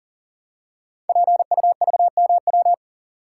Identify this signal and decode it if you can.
Morse code, heard as a keyed tone, about 39 wpm, 700 Hz. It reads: PUVMW
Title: Code Groups - 5 Letters - 40wpm